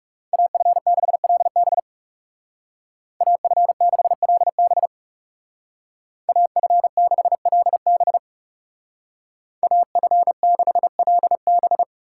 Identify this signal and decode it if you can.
Morse code, heard as a keyed tone, about 36 wpm, 700 Hz. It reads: AF6LB AF6LB AF6LB AF6LB
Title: Speed Racing - USA 2x2 Call Signs 30wpm